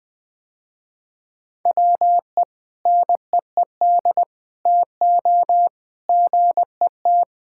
Morse code, heard as a keyed tone, about 21 wpm, 700 Hz. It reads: WE NEED TO GET
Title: Sentences with 5-Word Phrases - Spoken Beginning - 20wpm